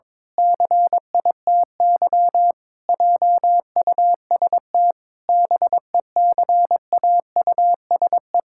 Morse code, heard as a keyed tone, about 22 wpm, 700 Hz. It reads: CITY JUST BECAUSE
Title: Pangram Challenge v2 - 22wpm